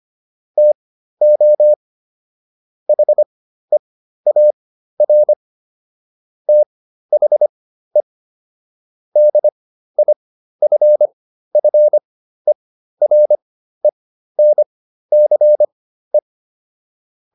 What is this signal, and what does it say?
Morse code, heard as a keyed tone, about 25 wpm, spaced as for 13 wpm, 600 Hz. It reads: TO HEAR THE DIFFERENCE